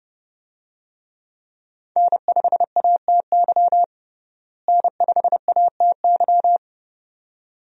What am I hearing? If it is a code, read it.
Morse code, heard as a keyed tone, about 30 wpm, 700 Hz. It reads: N5ATY N5ATY
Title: USA Call Signs - 5 Characters - Repeated 2x - 30wpm